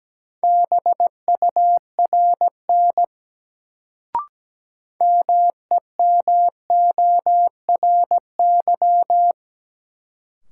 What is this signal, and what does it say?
Morse code, heard as a keyed tone, about 17 wpm, 700 Hz. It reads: BURN MEMORY
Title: Sentences from Top 1300 Words - New Word Drill Alt - 17wpm